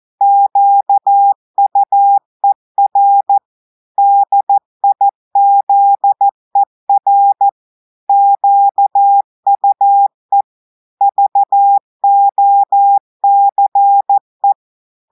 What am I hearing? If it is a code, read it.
Morse code, heard as a keyed tone, about 14 wpm, 800 Hz. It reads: QUER DIZER QUE VOCE
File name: audios / portugais / portugais texte en clair 2 vitesse 14 mots minutes